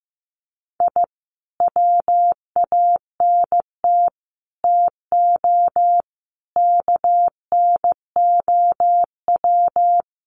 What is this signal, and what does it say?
Morse code, heard as a keyed tone, about 15 wpm, 700 Hz. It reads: I WANT TO KNOW
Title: Sets of 5 Words Encore2 15wpm